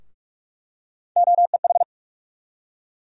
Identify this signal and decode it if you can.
Morse code, heard as a keyed tone, about 46 wpm, 700 Hz. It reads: OEH
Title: USA Call Sign Postfixes - 3 Letters 45wpm